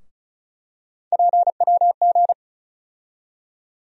Morse code, heard as a keyed tone, about 36 wpm, 700 Hz. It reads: PWG
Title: USA Call Sign Postfixes - 3 Letters 35wpm